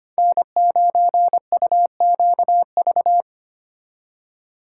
Morse code, heard as a keyed tone, about 30 wpm, 700 Hz. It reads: N9UQV
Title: Speed Racing - USA 1x3 Call Signs 25wpm